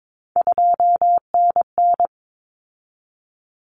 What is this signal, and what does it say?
Morse code, heard as a keyed tone, about 22 wpm, 700 Hz. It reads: 2NN
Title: Common Cut Numbers - 3-Digit 22wpm